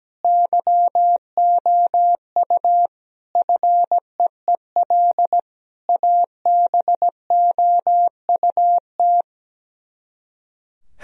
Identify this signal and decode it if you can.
Morse code, heard as a keyed tone, about 17 wpm, 700 Hz. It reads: YOU FEEL ABOUT
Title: Speed-Racing - Sets of 5 Words 17wpm